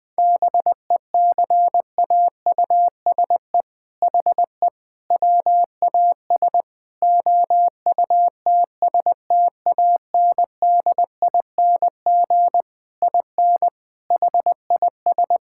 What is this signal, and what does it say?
Morse code, heard as a keyed tone, about 20 wpm, 700 Hz. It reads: BECAUSE HE WAS OUTSTANDING IN HIS
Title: All-Age Appropriate Jokes - 20wpm